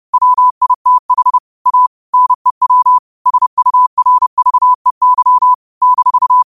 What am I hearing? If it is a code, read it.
Morse code, heard as a keyed tone, about 30 wpm, 1000 Hz. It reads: WITH A NEW SURVEY -